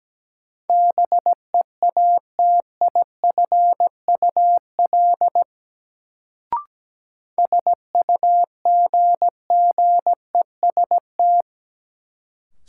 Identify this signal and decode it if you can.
Morse code, heard as a keyed tone, about 17 wpm, 700 Hz. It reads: BEATIFUL SUGGEST
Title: Sentences from Top 1300 Words - New Word Drill - 17wpm